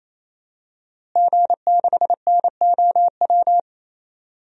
Morse code, heard as a keyed tone, about 27 wpm, 700 Hz. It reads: G6NOW
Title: DXCC-100 Call Signs - 5 Characters - Repeated 3x - 28wpm